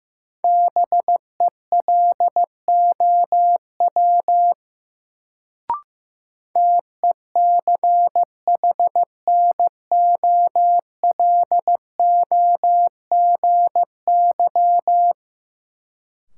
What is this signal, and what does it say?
Morse code, heard as a keyed tone, about 15 wpm, 700 Hz. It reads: BELOW TECHNOLOGY
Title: Sentences from Top 1000 Words - Review Part 5 - New Word Drill - 15wpm